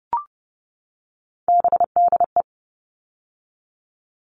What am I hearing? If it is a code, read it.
Morse code, heard as a keyed tone, about 30 wpm, 700 Hz. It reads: BDE
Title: All Cut Numbers - 3-Digit 30wpm